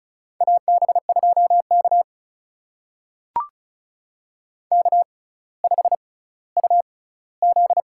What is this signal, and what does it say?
Morse code, heard as a keyed tone, about 36 wpm, spaced as for 18 wpm, 700 Hz. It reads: AB2K K5UZ
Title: USA Call Signs - 4 Characters - 6x Character Spacing - 35wpm